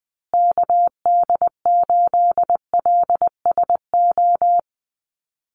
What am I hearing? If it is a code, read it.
Morse code, heard as a keyed tone, about 20 wpm, 700 Hz. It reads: KD8LSO
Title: USA Call Signs Repeated 2x 20wpm